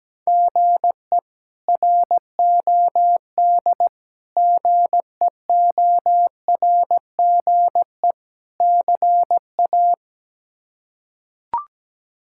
Morse code, heard as a keyed tone, about 17 wpm, 700 Hz. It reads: GE ROD GEORGE CA
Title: K1USN SST QSO Practice - 17wpm